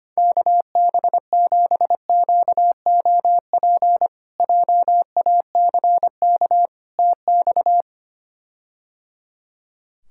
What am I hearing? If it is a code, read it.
Morse code, heard as a keyed tone, about 25 wpm, 700 Hz. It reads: KB7QOP JACK TX
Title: K1USN SST QSO Practice - 25wpm